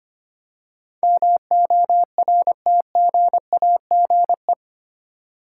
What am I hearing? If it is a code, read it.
Morse code, heard as a keyed tone, about 25 wpm, 700 Hz. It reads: MORTGAGE
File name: Eight Letter Words - 25wpm